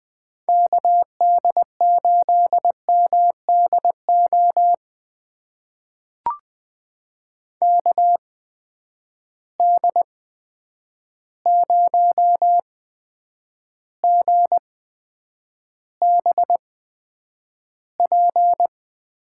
Morse code, heard as a keyed tone, about 20 wpm, spaced as for 9 wpm, 700 Hz. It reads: KD8MDO KD0GBP
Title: USA Call Signs - 6 Characters - 8x Character Spacing - 20wpm